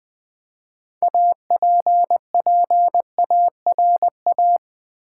Morse code, heard as a keyed tone, about 20 wpm, 700 Hz. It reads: APPARA
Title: Build A Word - 3 Letters At A Time - 9-Letter Words - 20wpm